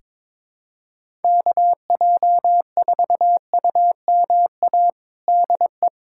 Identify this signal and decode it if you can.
Morse code, heard as a keyed tone, about 22 wpm, 700 Hz. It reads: KJ4UMA DE
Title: USA Callsign DE Callsign - 6 Characters - 22wpm